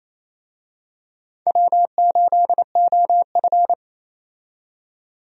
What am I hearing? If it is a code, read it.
Morse code, heard as a keyed tone, about 28 wpm, 700 Hz. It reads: W8OF
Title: USA Call Signs - 4 Characters - 28wpm